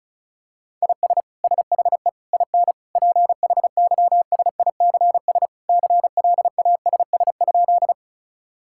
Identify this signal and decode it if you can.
Morse code, heard as a keyed tone, about 36 wpm, 700 Hz. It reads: IS SHE IN PHYSICS CLASS?
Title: Sentences from Top 1000 Words - Review Part 5 - 35wpm